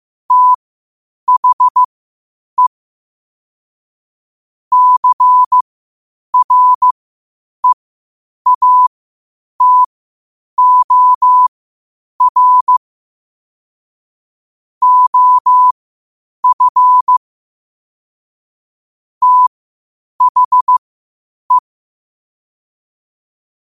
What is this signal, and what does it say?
Morse code, heard as a keyed tone, about 15 wpm, spaced as for 8 wpm, 1000 Hz. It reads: THE CREATOR OF THE
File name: qotdfiles / 8wpm